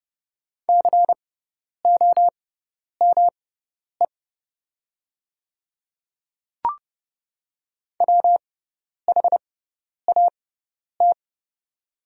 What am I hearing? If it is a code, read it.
Morse code, heard as a keyed tone, about 30 wpm, spaced as for 10 wpm, 700 Hz. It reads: COME WHAT
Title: 6x Character Spacing - Top 100 Words - 30wpm